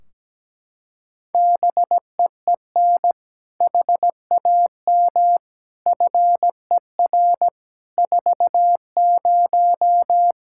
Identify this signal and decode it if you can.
Morse code, heard as a keyed tone, about 17 wpm, 700 Hz. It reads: BEEN HAM FER 40
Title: Practice QSO - Exchange 3 - More Info - 17wpm